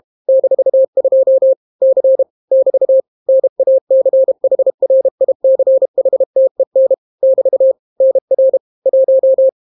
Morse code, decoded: -2 C <BT> NACHRICHTEN <BT> NR 1